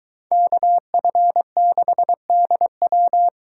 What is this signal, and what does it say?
Morse code, heard as a keyed tone, about 23 wpm, 700 Hz. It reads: KF6DW